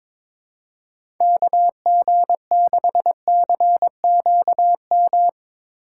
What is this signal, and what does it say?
Morse code, heard as a keyed tone, about 22 wpm, 700 Hz. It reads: KG6CQM
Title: USA Call Signs - 6 Characters - 2x Character Spacing - 22wpm